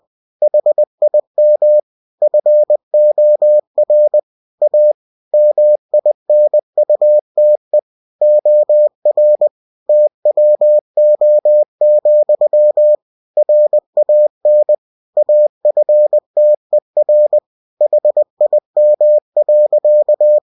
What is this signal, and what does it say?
Morse code, heard as a keyed tone, about 20 wpm, 600 Hz. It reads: HIM FOR A MINUTE OR TWO, RAN AFTER HIM.
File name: cw-books / winnie-the-pooh / winnie-20wpm-page-0022